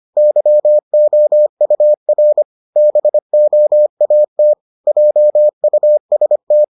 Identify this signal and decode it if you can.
Morse code, heard as a keyed tone, about 25 wpm, 600 Hz. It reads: YOUR BOAT JUST